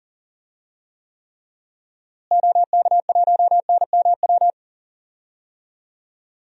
Morse code, heard as a keyed tone, about 40 wpm, 700 Hz. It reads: OK1NMW
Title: DXCC-100 Call Signs - 6 Characters - Repeated 3x - 40wpm